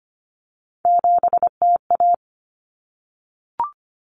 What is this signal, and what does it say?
Morse code, heard as a keyed tone, about 25 wpm, 700 Hz. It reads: 7TA
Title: Common Cut Numbers - 3-Digit 25wpm